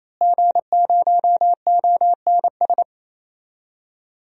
Morse code, heard as a keyed tone, about 28 wpm, 700 Hz. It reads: G0ONS
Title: DXCC-100 Call Signs - 5 Characters - Repeated 3x - 28wpm